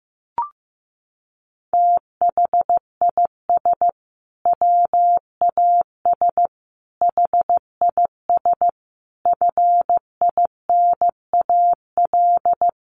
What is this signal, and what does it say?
Morse code, heard as a keyed tone, about 15 wpm, 700 Hz. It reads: THIS WAS HIS FINAL
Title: Sentences from Top 1300 Words 15wpm